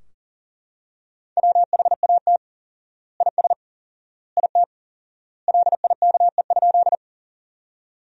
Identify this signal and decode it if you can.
Morse code, heard as a keyed tone, about 40 wpm, 700 Hz. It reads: WHAT IS IT LIKE?
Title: 4x Word-Spacing - Sentences from Top 100 Words - 40wpm